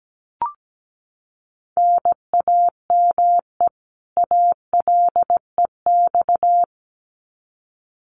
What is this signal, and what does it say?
Morse code, heard as a keyed tone, about 17 wpm, 700 Hz. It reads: NAME ALEX
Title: Common QSO Phrases 17wpm